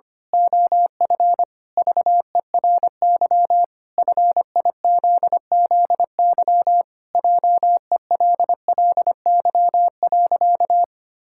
Morse code, heard as a keyed tone, about 25 wpm, 700 Hz. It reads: OF VERY FIZZY JELLY.
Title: Pangram Challenge v2 - 25wpm